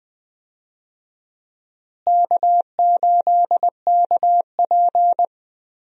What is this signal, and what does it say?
Morse code, heard as a keyed tone, about 20 wpm, 700 Hz. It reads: K8KP
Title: USA Call Signs - 4 Characters - Repeated 3x - 20wpm